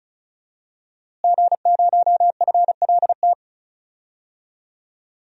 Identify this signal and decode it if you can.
Morse code, heard as a keyed tone, about 36 wpm, 700 Hz. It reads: G0FLT
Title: DXCC-100 Call Signs - 5 Characters - Repeated 2x - 35wpm